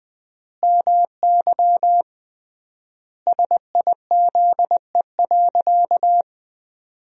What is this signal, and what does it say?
Morse code, heard as a keyed tone, about 20 wpm, 700 Hz. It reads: MY SIZE.
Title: 3x Word-Spacing - Sentences from Top 400 Words - 20wpm